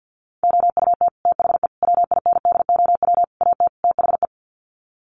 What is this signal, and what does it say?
Morse code, heard as a keyed tone, about 50 wpm, 700 Hz. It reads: OUT THE WINDOW AT THE
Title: Sets of 5 Words Encore2 50wpm